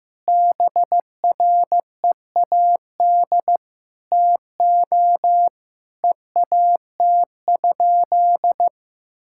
Morse code, heard as a keyed tone, about 15 wpm, 700 Hz. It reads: BREAD TO EAT?
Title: Sentences from Top 1600 Words 15wpm